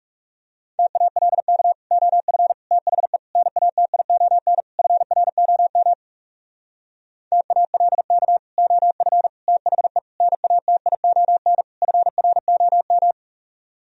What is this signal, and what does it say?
Morse code, heard as a keyed tone, about 40 wpm, 700 Hz. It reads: TALK OF THE NATION FROM TALK OF THE NATION FROM